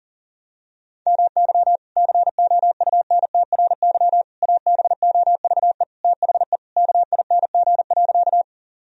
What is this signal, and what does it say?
Morse code, heard as a keyed tone, about 37 wpm, 700 Hz. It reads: MY COUNTRY ABOVE THE KING.
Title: Sentences from Top 1000 Words - Review Part 2 - 40wpm